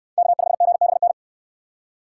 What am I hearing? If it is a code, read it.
Morse code, heard as a keyed tone, about 62 wpm, 700 Hz. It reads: K5GBN